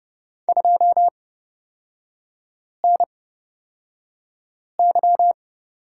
Morse code, heard as a keyed tone, about 30 wpm, spaced as for 5 wpm, 700 Hz. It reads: 2NY
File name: Lesson 37e - Letter x - Three Characters 30wpm